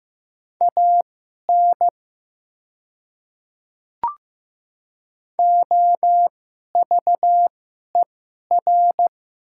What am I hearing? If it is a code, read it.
Morse code, heard as a keyed tone, about 15 wpm, spaced as for 11 wpm, 700 Hz. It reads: AN OVER
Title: 2x Character Spacing - Top 100 Words - 15wpm